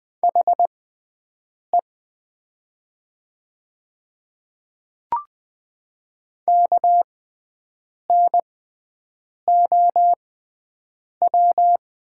Morse code, heard as a keyed tone, about 20 wpm, spaced as for 7 wpm, 700 Hz. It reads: HE KNOW
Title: 6x Character Spacing - Top 100 Words - 20wpm